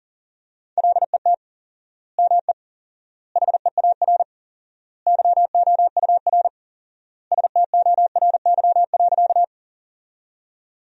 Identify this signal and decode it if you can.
Morse code, heard as a keyed tone, about 40 wpm, 700 Hz. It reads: LET ME HEAR YOUR STORY.
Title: 4x Word-Spacing - Sentences from Top 300 Words - 40wpm